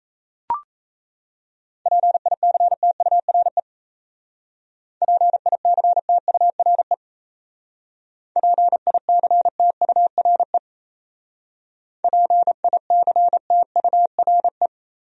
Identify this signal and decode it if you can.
Morse code, heard as a keyed tone, about 34 wpm, 700 Hz. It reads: PICTURE PICTURE PICTURE PICTURE
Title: Speed-Racing Top 200 Words 28wpm